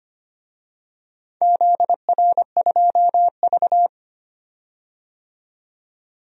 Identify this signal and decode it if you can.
Morse code, heard as a keyed tone, about 25 wpm, 700 Hz. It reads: ZR2V
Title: DXCC-100 Call Signs - 4 Characters - 25wpm